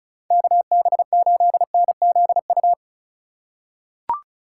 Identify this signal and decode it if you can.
Morse code, heard as a keyed tone, about 34 wpm, 700 Hz. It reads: KB8NZU
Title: USA Call Signs - 2x Character Spacing - 35wpm